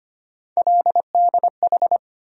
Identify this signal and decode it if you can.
Morse code, heard as a keyed tone, about 25 wpm, 700 Hz. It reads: LDH
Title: USA Call Sign Postfixes - 3 Letters 25wpm